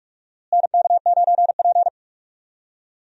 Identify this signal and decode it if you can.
Morse code, heard as a keyed tone, about 41 wpm, 700 Hz. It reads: NK9P
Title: USA Call Signs - 4 Characters - Repeated 2x - 45wpm